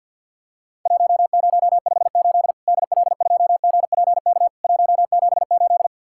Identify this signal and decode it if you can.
Morse code, heard as a keyed tone, about 50 wpm, 700 Hz. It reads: 1058 DL2GPK 178
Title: Worked All Europe DX Contest - QTC Exchange - 50wpm